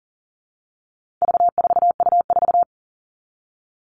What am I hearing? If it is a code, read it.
Morse code, heard as a keyed tone, about 40 wpm, 700 Hz. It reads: V4U4
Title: All Cut Numbers - 4-Digit 40wpm